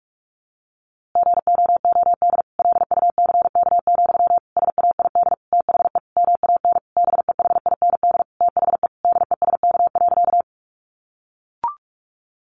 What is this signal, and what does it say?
Morse code, heard as a keyed tone, about 44 wpm, 700 Hz. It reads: GOOD LUCK, SAID THE MAN BEHIND THE DESK.
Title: A Sound Of Thunder 45wpm